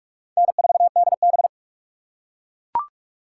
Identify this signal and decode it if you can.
Morse code, heard as a keyed tone, about 42 wpm, 700 Hz. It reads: N4DB